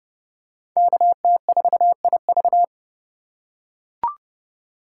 Morse code, heard as a keyed tone, about 30 wpm, 700 Hz. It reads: KT4IV